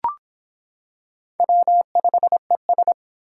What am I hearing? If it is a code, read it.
Morse code, heard as a keyed tone, about 25 wpm, 700 Hz. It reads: W5ES